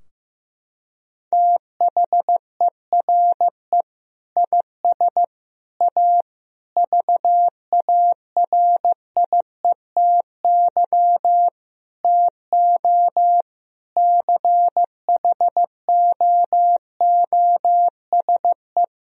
Morse code, heard as a keyed tone, about 16 wpm, 700 Hz. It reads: THERE IS A VARIETY TO CHOOSE
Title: Sentences from Top 1000 Words - Review Part 4 - 15wpm